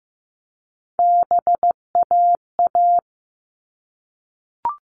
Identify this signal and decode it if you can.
Morse code, heard as a keyed tone, about 15 wpm, 700 Hz. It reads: BAA
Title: All Cut Numbers - 3-Digit 15wpm